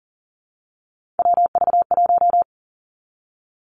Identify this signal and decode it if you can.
Morse code, heard as a keyed tone, about 40 wpm, 700 Hz. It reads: WV1
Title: Three Challenging Letter-Number Combinations 40wpm